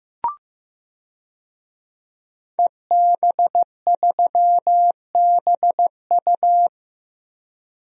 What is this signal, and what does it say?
Morse code, heard as a keyed tone, about 15 wpm, 700 Hz. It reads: EB3BU
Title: DXCC-100 Call Signs - 5 Characters - Repeated 2x - 15wpm